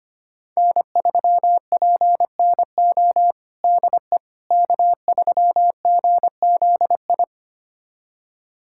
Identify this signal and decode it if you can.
Morse code, heard as a keyed tone, about 25 wpm, 700 Hz. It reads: N3PNO DE K3GZI
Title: USA Callsign DE Callsign - 5 Characters - 25wpm